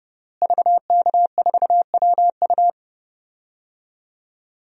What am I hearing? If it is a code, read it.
Morse code, heard as a keyed tone, about 30 wpm, 700 Hz. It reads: VK4WU
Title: DXCC-100 Call Signs - 5 Characters - Repeated 3x - 30wpm